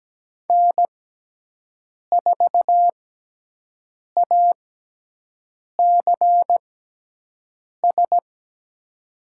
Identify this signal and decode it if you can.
Morse code, heard as a keyed tone, about 17 wpm, spaced as for 6 wpm, 700 Hz. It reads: N4ACS
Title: USA Call Signs - 5 Characters - 6x Character Spacing - 17wpm